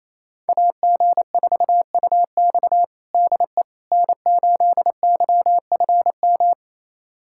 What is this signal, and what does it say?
Morse code, heard as a keyed tone, about 28 wpm, 700 Hz. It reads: AG4UX DE N8YFM
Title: USA Callsign DE Callsign - 5 Characters - 28wpm